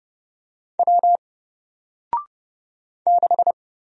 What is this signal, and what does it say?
Morse code, heard as a keyed tone, about 29 wpm, 700 Hz. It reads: W 6
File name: Single-Letter-Number-30wpm